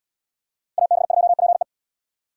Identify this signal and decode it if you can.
Morse code, heard as a keyed tone, about 70 wpm, 700 Hz. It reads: AB2PE